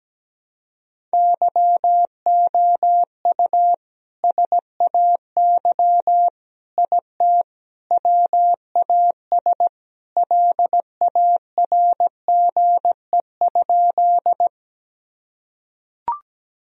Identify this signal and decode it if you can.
Morse code, heard as a keyed tone, about 17 wpm, 700 Hz. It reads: YOU SAY IT WAS LARGE?
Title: Sentences from Top 1000 Words - Review Part 1 - 17wpm